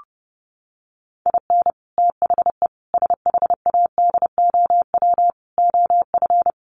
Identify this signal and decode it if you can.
Morse code, heard as a keyed tone, about 30 wpm, 700 Hz. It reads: IN THE SHADOW OF
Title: Sets of 5 Words Encore 30wpm